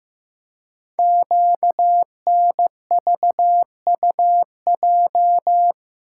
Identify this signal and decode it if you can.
Morse code, heard as a keyed tone, about 15 wpm, 700 Hz. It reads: QNVUJ
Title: Code Groups - 5 Letters - 15wpm